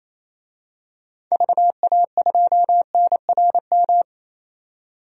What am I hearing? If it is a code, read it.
Morse code, heard as a keyed tone, about 28 wpm, 700 Hz. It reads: VA2NRM
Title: DXCC-100 Call Signs - 6 Characters - 28wpm